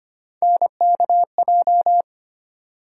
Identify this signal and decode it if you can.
Morse code, heard as a keyed tone, about 25 wpm, 700 Hz. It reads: NKJ
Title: USA Call Sign Postfixes - 3 Letters 25wpm